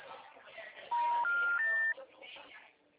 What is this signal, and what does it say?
Special information tone, with speech.
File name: out- -2010- - - .2